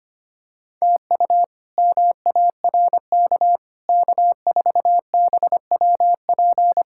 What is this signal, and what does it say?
Morse code, heard as a keyed tone, about 25 wpm, 700 Hz. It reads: TU MARK K4BWP
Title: K1USN SST QSO Practice - 25wpm